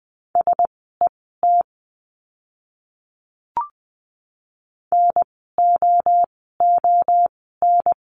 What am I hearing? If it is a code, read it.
Morse code, heard as a keyed tone, about 20 wpm, spaced as for 15 wpm, 700 Hz. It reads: SET NOON
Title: 036 Lesson 7f - Letter s - Words 20wpm